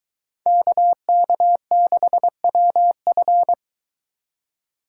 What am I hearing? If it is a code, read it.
Morse code, heard as a keyed tone, about 23 wpm, 700 Hz. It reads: KK6WF